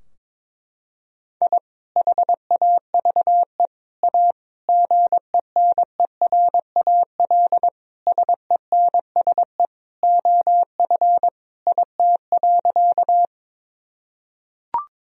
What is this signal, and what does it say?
Morse code, heard as a keyed tone, about 22 wpm, 700 Hz. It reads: I HAVE A GENERAL SENSE OF IT.
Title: Sentences from Top 1000 Words - Review Part 4 - 22wpm